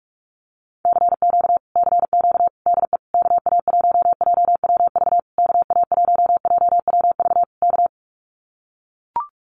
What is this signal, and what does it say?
Morse code, heard as a keyed tone, about 44 wpm, 700 Hz. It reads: CQ CQ DE KA1JWV KA1JWV K
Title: Common QSO Phrases Encore 45wpm